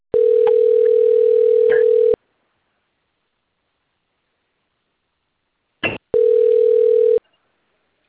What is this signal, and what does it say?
Ringback tone (North America), 440 + 480 Hz.